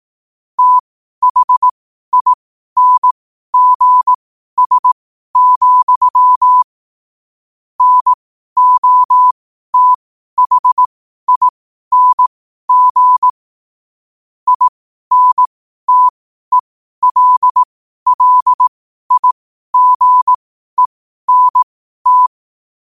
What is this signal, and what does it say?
Morse code, heard as a keyed tone, about 18 wpm, spaced as for 12 wpm, 1000 Hz. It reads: THINGS, NOTHING INTELLIGENT
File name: qotdfiles / 12wpm